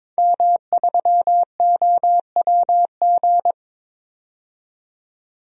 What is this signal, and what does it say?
Morse code, heard as a keyed tone, about 22 wpm, 700 Hz. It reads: M3OWG
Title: DXCC-100 Call Signs - 5 Characters - 22wpm